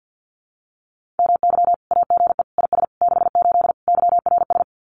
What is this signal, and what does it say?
Morse code, heard as a keyed tone, about 50 wpm, 700 Hz. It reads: MY AGE IS 68 YRS
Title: Common QSO Phrases Encore 50wpm